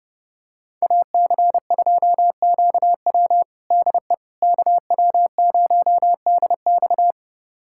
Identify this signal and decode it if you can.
Morse code, heard as a keyed tone, about 30 wpm, 700 Hz. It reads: AC2QW DE KW0DX
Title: USA Callsign DE Callsign - 5 Characters - 30wpm